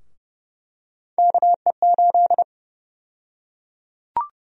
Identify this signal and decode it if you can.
Morse code, heard as a keyed tone, about 29 wpm, 700 Hz. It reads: KE8
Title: USA Call Sign Prefixes - 30wpm